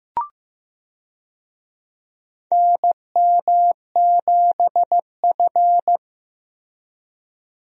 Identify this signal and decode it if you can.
Morse code, heard as a keyed tone, about 15 wpm, 700 Hz. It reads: NM7F
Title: USA Call Signs - 4 Characters - Repeated 3x - 15wpm